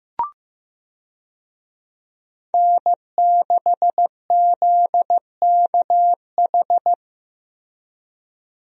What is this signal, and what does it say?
Morse code, heard as a keyed tone, about 15 wpm, 700 Hz. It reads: N6ZKH
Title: USA Call Signs - 5 Characters - 15wpm